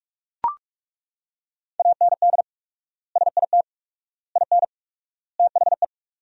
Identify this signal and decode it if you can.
Morse code, heard as a keyed tone, about 41 wpm, 700 Hz. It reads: AND SIT IN THE
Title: 4x Word-Spacing - Sentences from Top 500 Words - 45wpm